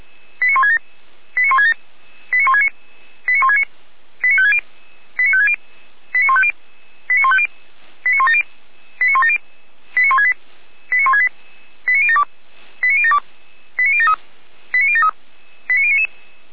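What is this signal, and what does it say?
Five-tone selective call (ZVEI), heard as quick runs of five tones, tones of about 70 ms. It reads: 89167 89167 89169 89169 89560 89560 89150 89150 89170 89170 89168 89168 89072 89072 89073 89073
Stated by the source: ZVEI five-tone call received by SDR, from the fire brigade Feuerwehr Orlamünde server.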